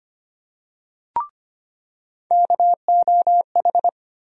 Morse code, heard as a keyed tone, about 25 wpm, 700 Hz. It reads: KOH
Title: USA Call Sign Postfixes - 3 Letters 25wpm